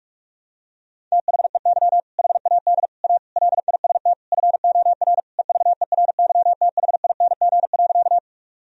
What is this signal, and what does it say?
Morse code, heard as a keyed tone, about 46 wpm, 700 Hz. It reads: THEY HAD A LIST FOR EVERYTHING.
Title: Sentences from Top 1000 Words - Review Part 2 - 45wpm